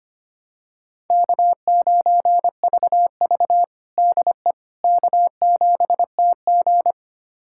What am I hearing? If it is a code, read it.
Morse code, heard as a keyed tone, about 25 wpm, 700 Hz. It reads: K9VV DE K7TG
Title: USA Callsign DE Callsign - 4 Characters - 25wpm